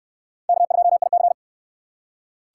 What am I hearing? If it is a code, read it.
Morse code, heard as a keyed tone, about 68 wpm, 700 Hz. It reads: K2IC